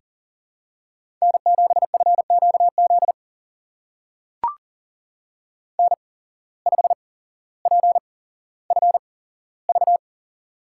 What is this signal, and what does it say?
Morse code, heard as a keyed tone, about 39 wpm, spaced as for 17 wpm, 700 Hz. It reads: N7FQZ N5PFV
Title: USA Call Signs - 5 Characters - 8x Character Spacing - 40wpm